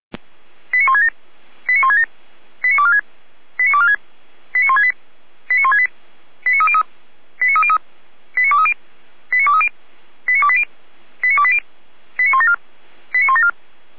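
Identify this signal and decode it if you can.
Five-tone selective call (ZVEI), heard as quick runs of five tones, tones of about 70 ms. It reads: 89167 89167 89246 89246 89168 89168 89303 89303 89230 89230 89280 89280 89174 89174
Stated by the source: ZVEI five-tone call 89173 received by SDR, archived on server of Feuerwehr Orlamünde (audio.f85360.feuerwehr-orlamuende.org).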